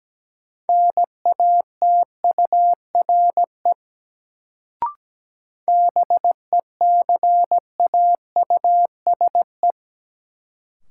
Morse code, heard as a keyed tone, about 17 wpm, 700 Hz. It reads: NATURE BECAUSE
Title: Sentences from Top 600 Words - New Word Drill - 17wpm